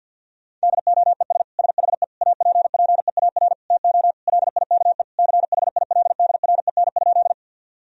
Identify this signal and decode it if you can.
Morse code, heard as a keyed tone, about 52 wpm, 700 Hz. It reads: DOES SHE APPEAR TO LIKE CHILDREN?
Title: Sentences from Top 1000 Words - Review Part 2 - 50wpm